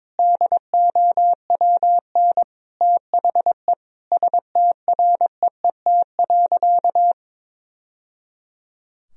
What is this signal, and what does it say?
Morse code, heard as a keyed tone, about 21 wpm, 700 Hz. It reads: DOWN THE STREET.
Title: Sentences from Top 1000 Words - Review Part 3 - 22wpm